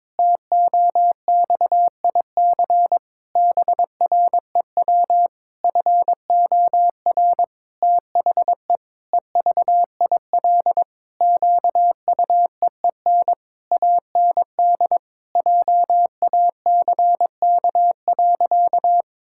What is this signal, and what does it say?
Morse code, heard as a keyed tone, about 22 wpm, 700 Hz. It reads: TOXIC BREW FOR THE EVIL QUEEN AND JACK.
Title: Pangram Challenge v2 - 22wpm